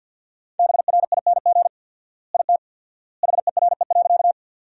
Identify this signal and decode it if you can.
Morse code, heard as a keyed tone, about 49 wpm, 700 Hz. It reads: BRING IT HERE.
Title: 4x Word-Spacing - Sentences from Top 500 Words - 50wpm